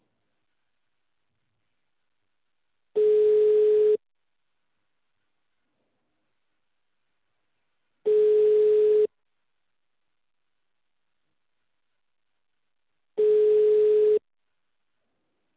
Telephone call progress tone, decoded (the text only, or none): ringback tone